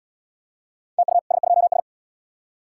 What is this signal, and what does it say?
Morse code, heard as a keyed tone, about 80 wpm, 700 Hz. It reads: TU S2S